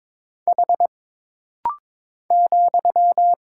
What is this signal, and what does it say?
Morse code, heard as a keyed tone, about 22 wpm, 700 Hz. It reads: H ,
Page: This rapid-fire timing is geared to push your ICR (Instant Character Recognition) skills.